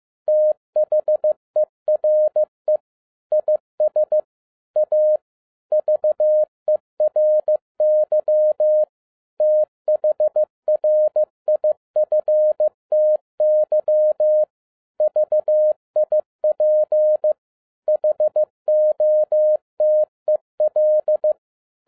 Morse code, heard as a keyed tone, about 15 wpm, 610 Hz. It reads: THERE IS A VERY THRIFTY VIP HOTEL
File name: 2011-07-mp3 / n12 T8XO 15wpm